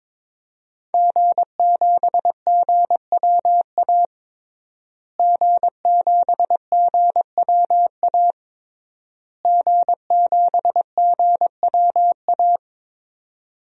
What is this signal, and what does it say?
Morse code, heard as a keyed tone, about 22 wpm, 700 Hz. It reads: G7GWA G7GWA G7GWA
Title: DXCC-100 Call Signs - 5 Characters - Repeated 3x - 22wpm